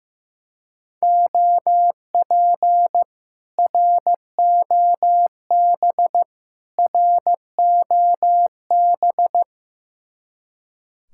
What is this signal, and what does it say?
Morse code, heard as a keyed tone, about 15 wpm, 700 Hz. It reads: OP ROB ROB
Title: Speed-Racing Common QSO Phrases Encore - 15wpm